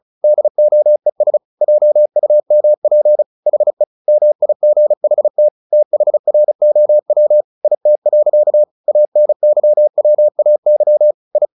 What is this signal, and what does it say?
Morse code, heard as a keyed tone, about 34 wpm, 600 Hz. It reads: DOES JUMP HE MIGHT THROW IT. ANYWAY I
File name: the old man and the sea / the old man and the sea 1 35wpm